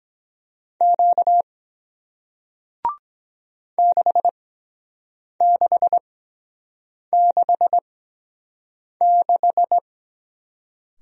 Morse code, heard as a keyed tone, about 23 wpm, spaced as for 7 wpm, 700 Hz. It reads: Q 6666